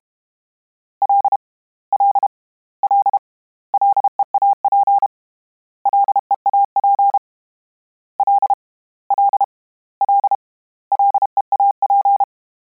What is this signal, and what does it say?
Morse code, heard as a keyed tone, about 32 wpm, 800 Hz. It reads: L L L LEAP LEAP L L L LEAP
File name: Combinations / Combinations-mp3s / Combinations@30